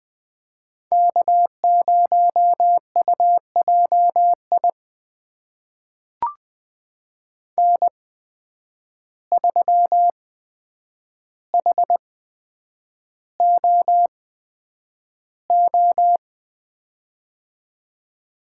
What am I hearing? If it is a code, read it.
Morse code, heard as a keyed tone, about 20 wpm, spaced as for 9 wpm, 700 Hz. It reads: K0UJI N3HOO
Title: USA Call Signs - 5 Characters - 8x Character Spacing - 20wpm